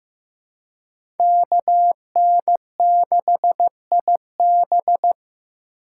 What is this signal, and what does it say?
Morse code, heard as a keyed tone, about 15 wpm, 700 Hz. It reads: KN6IB
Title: USA Call Signs - 5 Characters - 2x Character Spacing - 15wpm